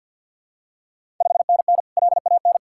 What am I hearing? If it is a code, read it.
Morse code, heard as a keyed tone, about 52 wpm, 690 Hz. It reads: 5NN LAN
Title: Ontario QSO Party Exchange 50wpm